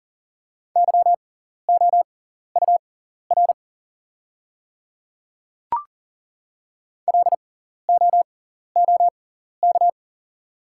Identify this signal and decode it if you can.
Morse code, heard as a keyed tone, about 39 wpm, spaced as for 14 wpm, 690 Hz. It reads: YOUR LOOK
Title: 6x Character Spacing - Top 100 Words - 40wpm